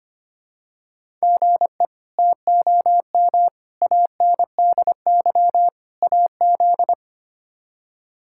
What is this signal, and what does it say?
Morse code, heard as a keyed tone, about 25 wpm, 700 Hz. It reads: GE TOM ANDY AZ
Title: K1USN SST QSO Practice - 25wpm